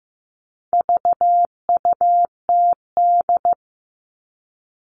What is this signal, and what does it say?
Morse code, heard as a keyed tone, about 15 wpm, 700 Hz. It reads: VUTD
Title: All Cut Numbers - 4-Digit 15wpm